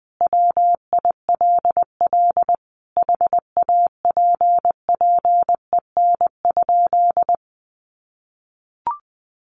Morse code, heard as a keyed tone, about 20 wpm, 700 Hz. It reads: WILL HAPPEN?
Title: Sentences from Top 1500 Words 20wpm